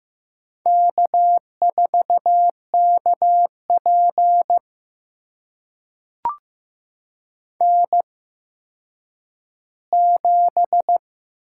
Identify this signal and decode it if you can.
Morse code, heard as a keyed tone, about 15 wpm, spaced as for 9 wpm, 700 Hz. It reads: K4KP N7
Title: USA Call Signs - 4 Characters - 8x Character Spacing - 15wpm